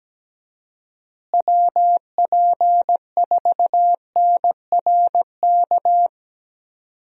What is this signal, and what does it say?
Morse code, heard as a keyed tone, about 17 wpm, 700 Hz. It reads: WP4NRK
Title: USA Call Signs - 6 Characters - Repeated 3x - 17wpm